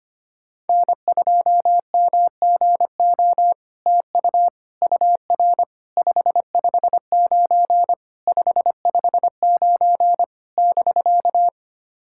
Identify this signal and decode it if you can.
Morse code, heard as a keyed tone, about 25 wpm, 700 Hz. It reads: N2MGO TU UR 559 559 <BK>
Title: SOTA QSO Practice 25wpm